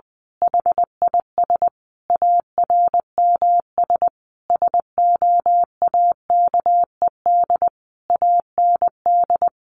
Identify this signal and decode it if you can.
Morse code, heard as a keyed tone, about 20 wpm, 700 Hz. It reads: HIS ARMS SOAKED AND
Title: A Sound Of Thunder 20wpm